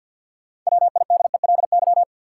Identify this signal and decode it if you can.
Morse code, heard as a keyed tone, about 51 wpm, 700 Hz. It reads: WIDELY
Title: Six Letter Words - 50wpm